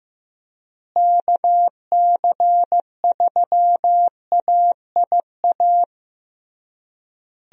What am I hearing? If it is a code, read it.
Morse code, heard as a keyed tone, about 16 wpm, 700 Hz. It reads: KC3AIA
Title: USA Call Signs - 6 Characters - Repeated 3x - 15wpm